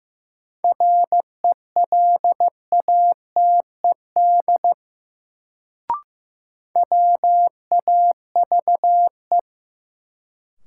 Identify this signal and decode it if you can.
Morse code, heard as a keyed tone, about 15 wpm, 700 Hz. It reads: RELATED WAVE
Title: Sentences from Top 1400 Words - New Word Drill Alt - 15wpm